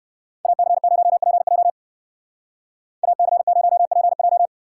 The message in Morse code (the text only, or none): AC0PJ AC0PJ